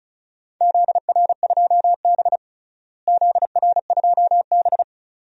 Morse code, heard as a keyed tone, about 36 wpm, 700 Hz. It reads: ZR2B ZR2B
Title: DXCC-100 Call Signs - 4 Characters - Repeated 2x - 35wpm